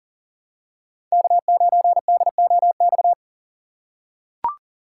Morse code, heard as a keyed tone, about 40 wpm, 700 Hz. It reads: K9DOX
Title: USA Call Signs - 5 Characters - 4x Character Spacing - 40wpm